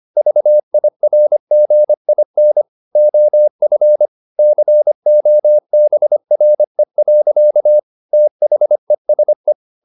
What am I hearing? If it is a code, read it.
Morse code, heard as a keyed tone, about 25 wpm, 600 Hz. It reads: VIRGIN OF COBRE. THESE
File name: the old man and the sea part 1 25wpm